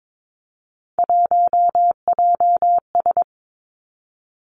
Morse code, heard as a keyed tone, about 22 wpm, 700 Hz. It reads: 1JS
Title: Three Challenging Letter-Number Combinations 22wpm